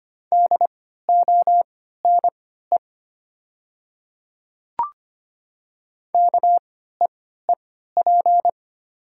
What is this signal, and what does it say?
Morse code, heard as a keyed tone, about 25 wpm, spaced as for 14 wpm, 700 Hz. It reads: DONE KEEP
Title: 3x Character Spacing - Top 500 Words - 25wpm